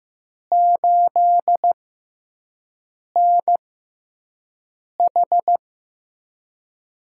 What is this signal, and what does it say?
Morse code, heard as a keyed tone, about 15 wpm, spaced as for 5 wpm, 700 Hz. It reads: 8NH
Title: USA Call Signs - 4 Characters - 6x Character Spacing - 15wpm